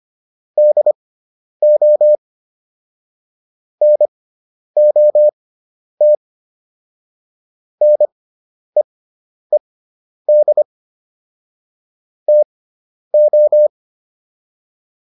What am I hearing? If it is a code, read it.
Morse code, heard as a keyed tone, about 25 wpm, spaced as for 10 wpm, 600 Hz. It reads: DO NOT NEED TO